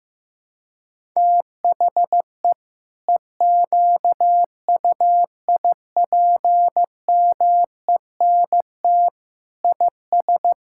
Morse code, heard as a keyed tone, about 16 wpm, 700 Hz. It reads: THE EQUIPMENT IS
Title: Sentences from Top 1000 Words - Review Part 5 - 15wpm